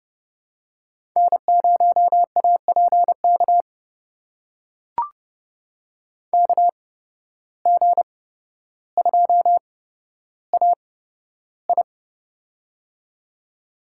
Morse code, heard as a keyed tone, about 30 wpm, spaced as for 13 wpm, 700 Hz. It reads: N0APK KG2AI
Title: USA Call Signs - 5 Characters - 8x Character Spacing - 30wpm